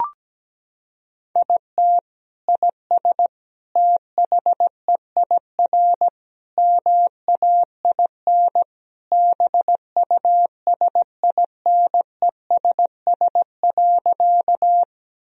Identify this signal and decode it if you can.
Morse code, heard as a keyed tone, about 17 wpm, 700 Hz. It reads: IT IS THEIR MAIN BUSINESS.
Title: Sentences from Top 1000 Words - Review Part 2 - 17wpm